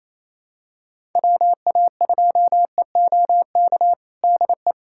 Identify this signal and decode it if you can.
Morse code, heard as a keyed tone, about 27 wpm, 700 Hz. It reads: WA2EOK DE
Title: USA Callsign DE Callsign - 6 Characters - 28wpm